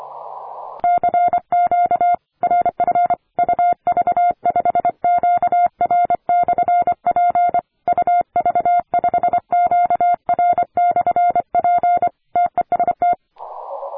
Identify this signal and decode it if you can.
Morse code, heard as a keyed tone, about 25 wpm, 700 Hz. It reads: CQ RF UV5QR/P UV5QR/P TEST
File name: attachments / 40m UU7JF P